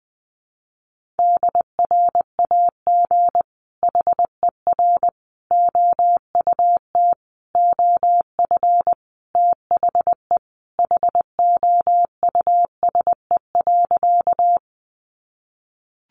Morse code, heard as a keyed tone, about 20 wpm, 700 Hz. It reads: DRAG HER OUT OF THE HOUSE.
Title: Sentences from Top 1500 Words 20wpm